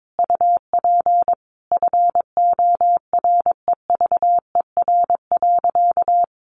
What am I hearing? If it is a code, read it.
Morse code, heard as a keyed tone, about 22 wpm, 700 Hz. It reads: UP FOREVER.
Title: A Sound Of Thunder 22wpm